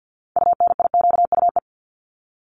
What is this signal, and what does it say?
Morse code, heard as a keyed tone, about 48 wpm, 700 Hz. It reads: UNIQUE
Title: Top 751-1000 Words 50wpm